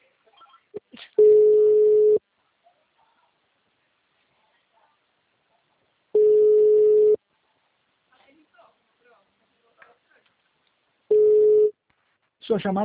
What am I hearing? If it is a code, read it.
Ringback tone (Europe, 425 Hz), with speech.